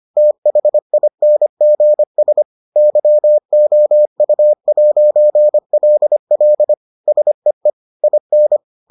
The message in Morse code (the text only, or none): THINGS YOU'LL SEE IN